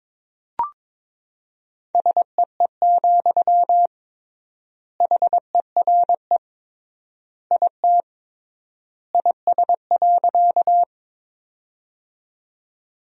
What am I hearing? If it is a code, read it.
Morse code, heard as a keyed tone, about 22 wpm, 700 Hz. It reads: SEE, HERE IT IS.
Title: 3x Word-Spacing - Sentences from Top 200 Words - 22wpm